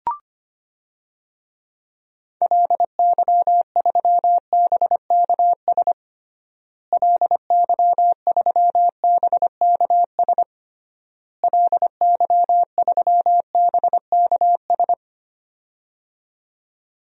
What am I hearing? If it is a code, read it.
Morse code, heard as a keyed tone, about 25 wpm, 700 Hz. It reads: LY3BKS LY3BKS LY3BKS
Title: DXCC-100 Call Signs - 6 Characters - Repeated 3x - 25wpm